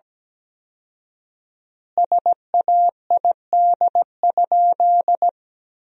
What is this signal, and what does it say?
Morse code, heard as a keyed tone, about 17 wpm, 700 Hz. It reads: SAID?
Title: 4x Word-Spacing - Sentences from Top 100 Words - 17wpm